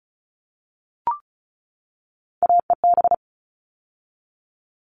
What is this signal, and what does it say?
Morse code, heard as a keyed tone, about 34 wpm, 700 Hz. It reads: AEB
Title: All Cut Numbers - 3-Digit 35wpm